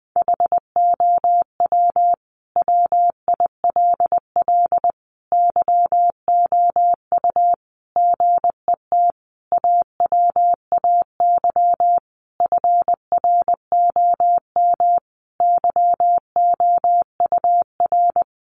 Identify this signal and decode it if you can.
Morse code, heard as a keyed tone, about 20 wpm, 700 Hz. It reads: HOW WILL YOU GET AWAY FROM YOUR
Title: Sentences from Top 1500 Words 20wpm